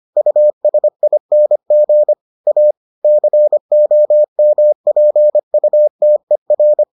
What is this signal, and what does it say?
Morse code, heard as a keyed tone, about 25 wpm, 600 Hz. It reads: USING A COMPUTER